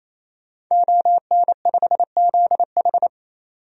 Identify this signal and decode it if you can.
Morse code, heard as a keyed tone, about 28 wpm, 700 Hz. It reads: ON5ZH
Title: DXCC-100 Call Signs - 5 Characters - 28wpm